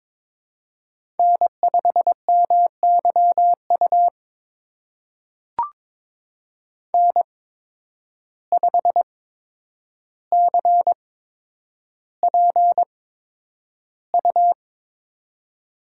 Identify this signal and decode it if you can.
Morse code, heard as a keyed tone, about 22 wpm, spaced as for 9 wpm, 700 Hz. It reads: N5MYU N5CPU
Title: USA Call Signs - 5 Characters - 8x Character Spacing - 22wpm